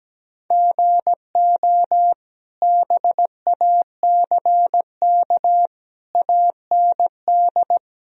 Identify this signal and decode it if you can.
Morse code, heard as a keyed tone, about 17 wpm, 700 Hz. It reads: GO BACK AND
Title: Sentences from Top 1000 Words - Review Part 1 - 17wpm